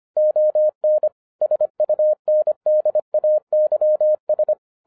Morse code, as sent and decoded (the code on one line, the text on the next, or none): --- -. / ... ..- -. -.. .- -.-- ...
ON SUNDAYS